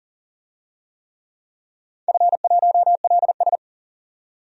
Morse code, heard as a keyed tone, about 39 wpm, 700 Hz. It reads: F1LS
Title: DXCC-100 Call Signs - 4 Characters - 40wpm